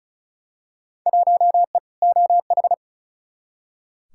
Morse code, heard as a keyed tone, about 36 wpm, 700 Hz. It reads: 1E OH
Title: Field Day Exchanges - 35wpm